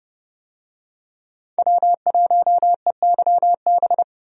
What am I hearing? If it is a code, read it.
Morse code, heard as a keyed tone, about 32 wpm, 700 Hz. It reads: W1EYB
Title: USA Call Signs - 5 Characters - 30wpm